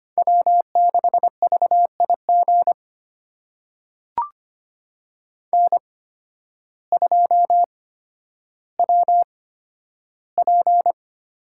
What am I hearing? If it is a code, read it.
Morse code, heard as a keyed tone, about 25 wpm, spaced as for 12 wpm, 700 Hz. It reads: W6VIG N2WP
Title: USA Call Signs - 8x Character Spacing - 25wpm